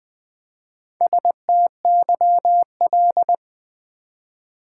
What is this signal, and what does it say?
Morse code, heard as a keyed tone, about 20 wpm, 690 Hz. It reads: STYL
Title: Build A Word - 1 Letter At A Time - 5-Letter Words - 20wpm